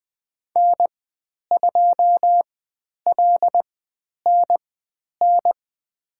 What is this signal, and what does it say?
Morse code, heard as a keyed tone, about 20 wpm, spaced as for 10 wpm, 700 Hz. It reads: N2LNN
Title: 102 Lesson 17g - Letter u - Call Signs 20wpm